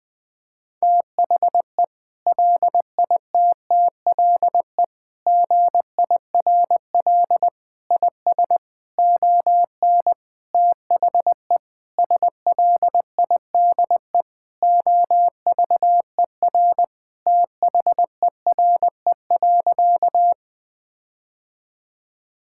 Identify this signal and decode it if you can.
Morse code, heard as a keyed tone, about 21 wpm, 700 Hz. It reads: THE LITTLE GIRL IS ON THE SLIDE OVER THERE.
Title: Sentences from Top 1600 Words 20wpm